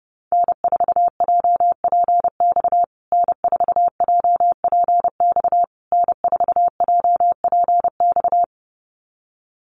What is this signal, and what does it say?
Morse code, heard as a keyed tone, about 30 wpm, 700 Hz. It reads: N4JPX N4JPX N4JPX
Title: USA Call Signs Repeated 3x 30wpm